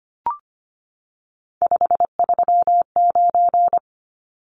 Morse code, heard as a keyed tone, about 25 wpm, 700 Hz. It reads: 539
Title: Numbers - 3-Digit 25wpm